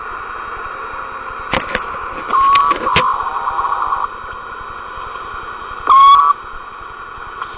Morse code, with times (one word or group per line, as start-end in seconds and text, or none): none